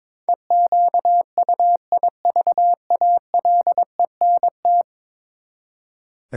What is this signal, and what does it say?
Morse code, heard as a keyed tone, about 22 wpm, 700 Hz. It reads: EQUIVALENT
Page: morse-code-ninja-practice-22wpm